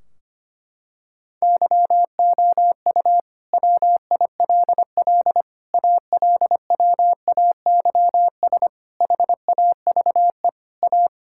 Morse code, heard as a keyed tone, about 25 wpm, 700 Hz. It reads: YOU WILL ALWAYS HAVE A
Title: Sentences from Top 1600 Words 25wpm